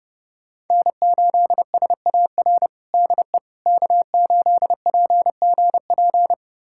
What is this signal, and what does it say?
Morse code, heard as a keyed tone, about 30 wpm, 700 Hz. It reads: N8SAR DE K8PGP
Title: USA Callsign DE Callsign - 5 Characters - 30wpm